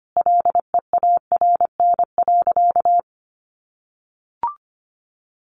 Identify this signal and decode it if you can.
Morse code, heard as a keyed tone, about 25 wpm, 700 Hz. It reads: LEARN.
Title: Sentences from Top 800 Words 25wpm